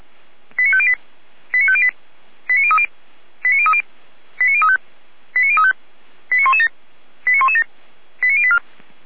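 Five-tone selective call (ZVEI), heard as quick runs of five tones, tones of about 80 ms. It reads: 89508 89508 89030 89030 89035 89035 89117 89117 89084
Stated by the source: SDR-received ZVEI five-tone call, code 89080.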